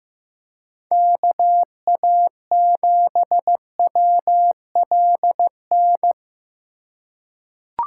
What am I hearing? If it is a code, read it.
Morse code, heard as a keyed tone, about 15 wpm, 700 Hz. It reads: KA7WLN
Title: USA Call Signs - 6 Characters - 3x Character Spacing - 15wpm